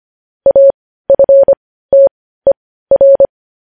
Morse code, heard as a keyed tone, about 25 wpm, spaced as for 15 wpm, 550 Hz. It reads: AFTER
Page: news headlines in morse code at 15 wpm